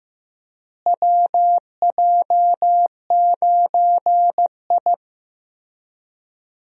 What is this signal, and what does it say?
Morse code, heard as a keyed tone, about 15 wpm, 700 Hz. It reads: WJ9I
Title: USA Call Signs - 4 Characters - Repeated 3x - 15wpm